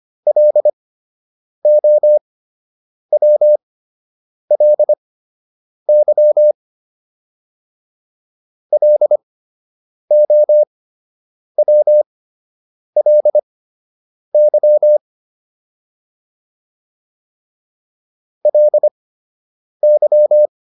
Morse code, heard as a keyed tone, about 25 wpm, spaced as for 8 wpm, 600 Hz. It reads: LOWLY LOWLY LY